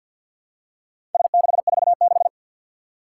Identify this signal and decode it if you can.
Morse code, heard as a keyed tone, about 52 wpm, 700 Hz. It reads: S646
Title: Four Challenging Letter-Number Combinations 50wpm